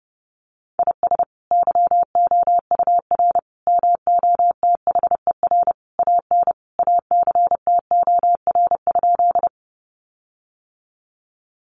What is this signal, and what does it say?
Morse code, heard as a keyed tone, about 30 wpm, 700 Hz. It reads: IS YOUR MOTHER AN ACTOR?
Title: Sentences from Top 1500 Words 30wpm